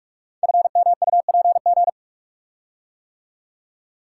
Morse code, heard as a keyed tone, about 44 wpm, 690 Hz. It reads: FMUPG
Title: Code Groups - 5 Letters - 45wpm